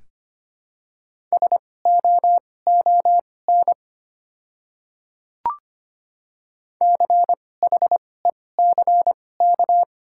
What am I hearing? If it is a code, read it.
Morse code, heard as a keyed tone, about 25 wpm, spaced as for 18 wpm, 700 Hz. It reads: SOON CHECK
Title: 2x Character Spacing - Top 500 Words - 25wpm